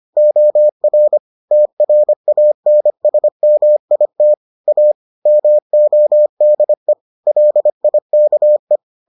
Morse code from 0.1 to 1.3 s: OR

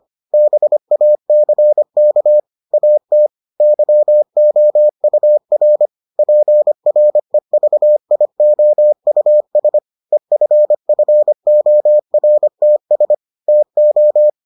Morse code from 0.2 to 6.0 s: BACK AT YOUR

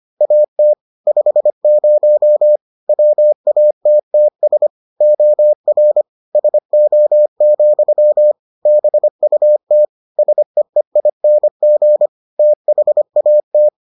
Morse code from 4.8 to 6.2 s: OR

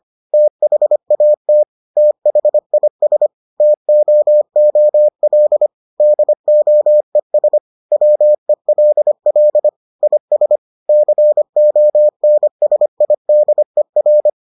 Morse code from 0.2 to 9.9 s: THAT THIS TOOL DOES WELL

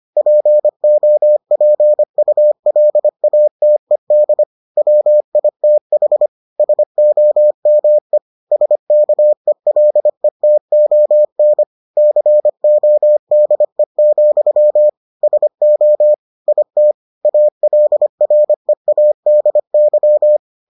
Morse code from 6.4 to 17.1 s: SOME SKELETON CODE, SO IT

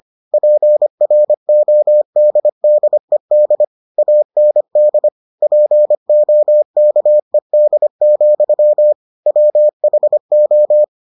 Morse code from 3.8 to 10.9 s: AND POKED, WHO

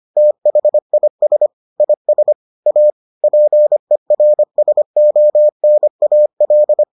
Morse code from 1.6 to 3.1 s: IS A

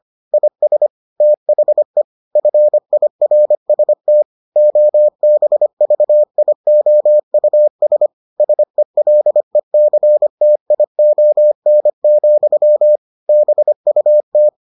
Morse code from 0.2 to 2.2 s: IS THE

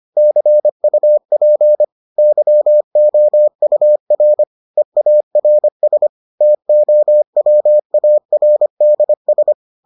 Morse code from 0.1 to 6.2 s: CUP YOUR EARS

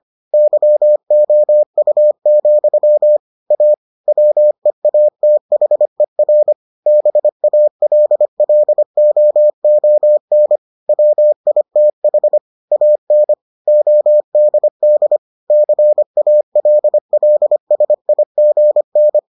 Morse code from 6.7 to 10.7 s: BALLOON

